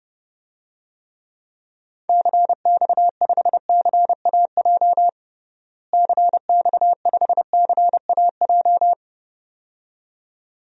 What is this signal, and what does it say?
Morse code, heard as a keyed tone, about 30 wpm, 700 Hz. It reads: CX5CAJ CX5CAJ
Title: DXCC-100 Call Signs - 6 Characters - Repeated 2x - 30wpm